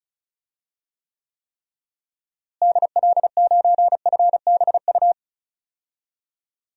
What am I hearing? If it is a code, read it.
Morse code, heard as a keyed tone, about 36 wpm, 700 Hz. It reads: DL9FBU
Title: DXCC-100 Call Signs - 6 Characters - Repeated 3x - 35wpm